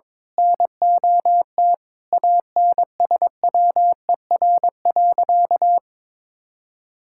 Morse code, heard as a keyed tone, about 22 wpm, 700 Hz. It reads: NOT ANSWER.
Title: Sentences from Top 1000 Words - Review Part 6 - 22wpm